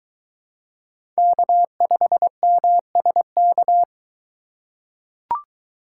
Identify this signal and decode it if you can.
Morse code, heard as a keyed tone, about 21 wpm, 700 Hz. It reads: K5MSK